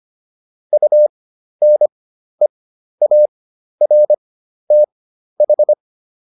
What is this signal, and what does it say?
Morse code, heard as a keyed tone, about 25 wpm, spaced as for 12 wpm, 600 Hz. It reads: UNEARTH